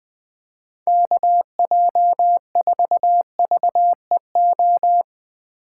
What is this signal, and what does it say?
Morse code, heard as a keyed tone, about 20 wpm, 700 Hz. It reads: KJ4VEO
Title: USA Call Signs - 3x Character Spacing - 20wpm